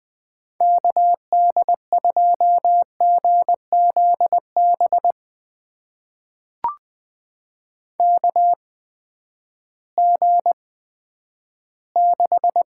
Morse code, heard as a keyed tone, about 20 wpm, spaced as for 11 wpm, 700 Hz. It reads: KD2GZB KG6
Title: USA Call Signs - 6 Characters - 8x Character Spacing - 20wpm